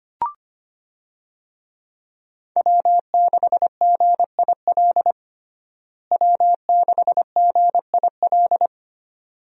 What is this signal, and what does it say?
Morse code, heard as a keyed tone, about 25 wpm, 700 Hz. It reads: W6GIL W6GIL
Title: USA Call Signs - 5 Characters - Repeated 2x - 25wpm